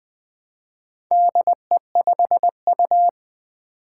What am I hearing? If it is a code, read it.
Morse code, heard as a keyed tone, about 20 wpm, 700 Hz. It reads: DE5U
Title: Four Challenging Letter-Number Combinations 20wpm